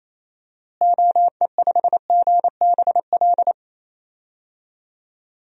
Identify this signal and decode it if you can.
Morse code, heard as a keyed tone, about 27 wpm, 700 Hz. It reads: OE5GBL
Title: DXCC-100 Call Signs - 6 Characters - Repeated 2x - 28wpm